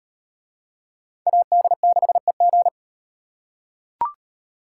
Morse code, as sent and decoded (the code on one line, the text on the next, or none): .- -.. -.... --.
AD6EG